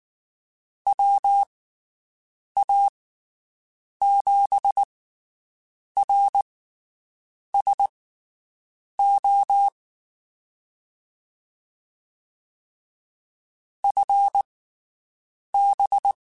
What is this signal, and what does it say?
Morse code, heard as a keyed tone, about 19 wpm, spaced as for 7 wpm, 800 Hz. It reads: WA7RSO FB
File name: (spaced) / QSOs Spaced-mp3s / WD0BGP 1 @18 sp